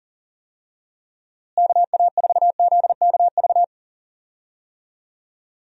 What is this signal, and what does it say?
Morse code, heard as a keyed tone, about 40 wpm, 700 Hz. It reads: KA4ZKV
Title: USA Call Signs - 6 Characters - Repeated 3x - 40wpm